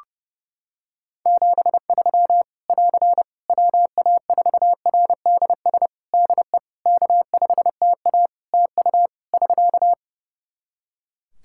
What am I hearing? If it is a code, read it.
Morse code, heard as a keyed tone, about 30 wpm, 700 Hz. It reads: 73 <AR> WA4RDS DE K5TA TU <SK>
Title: Practice QSO - Exchange 4 - Ending - 30wpm